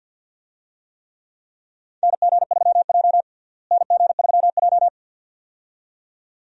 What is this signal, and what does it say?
Morse code, heard as a keyed tone, about 52 wpm, 700 Hz. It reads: NG3J NG3J
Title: USA Call Signs - 4 Characters - Repeated 2x - 50wpm